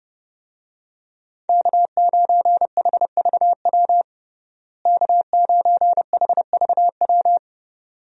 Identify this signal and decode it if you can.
Morse code, heard as a keyed tone, about 30 wpm, 700 Hz. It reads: K9HVW K9HVW
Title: USA Call Signs - 5 Characters - Repeated 2x - 30wpm